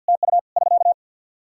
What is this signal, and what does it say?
Morse code, heard as a keyed tone, about 53 wpm, 700 Hz. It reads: TU <SK>
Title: Practice QSO - Exchange 4 - Ending - 50wpm